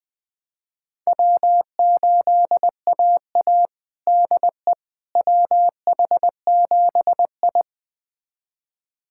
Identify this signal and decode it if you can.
Morse code, heard as a keyed tone, about 21 wpm, 700 Hz. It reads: W8AA DE WH7I
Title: USA Callsign DE Callsign - 4 Characters - 20wpm